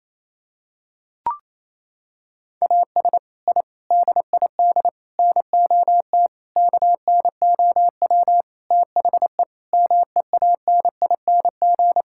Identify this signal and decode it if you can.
Morse code, heard as a keyed tone, about 28 wpm, 700 Hz. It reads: AS I DID NOT KNOW THE MEANING
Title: Sentences from Top 1000 Words - Review Part 6 - 28wpm